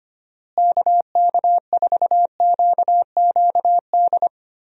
Morse code, heard as a keyed tone, about 25 wpm, 700 Hz. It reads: KK4QQD
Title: USA Call Signs - 6 Characters - Repeated 2x - 25wpm